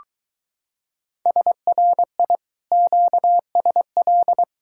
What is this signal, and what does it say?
Morse code, heard as a keyed tone, about 23 wpm, 700 Hz. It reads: SRI QSL